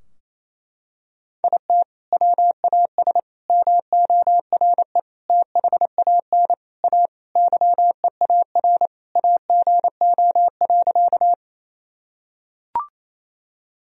Morse code, heard as a keyed tone, about 27 wpm, 700 Hz. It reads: IT WAS MORE THAN A YEAR AGO.
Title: Sentences from Top 1000 Words - Review Part 3 - 28wpm